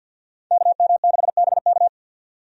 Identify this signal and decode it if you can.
Morse code, heard as a keyed tone, about 51 wpm, 700 Hz. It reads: KM6BK
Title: USA Call Signs - 5 Characters - 50wpm